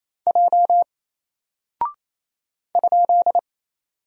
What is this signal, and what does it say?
Morse code, heard as a keyed tone, about 27 wpm, 700 Hz. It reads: J ?
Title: Single Character (Letter:Number:Punctuation:Prosign) 28wpm